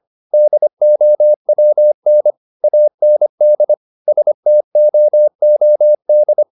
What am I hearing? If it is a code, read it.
Morse code, heard as a keyed tone, about 25 wpm, 600 Hz. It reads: DOWN AND STOOD